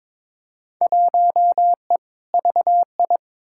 Morse code, heard as a keyed tone, about 22 wpm, 700 Hz. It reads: 1E VI
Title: Field Day Exchanges Encore - 22wpm